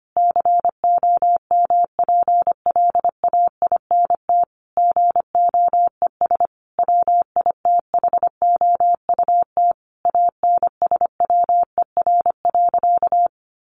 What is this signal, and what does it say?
Morse code, heard as a keyed tone, about 25 wpm, 700 Hz. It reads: COMPLAINT GOES WITHOUT ANSWER.
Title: Sentences from Top 1500 Words 25wpm